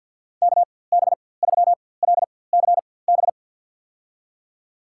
Morse code, heard as a keyed tone, about 50 wpm, spaced as for 23 wpm, 700 Hz. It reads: KB3LCB
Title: USA Call Signs - 6 Characters - 4x Character Spacing - 50wpm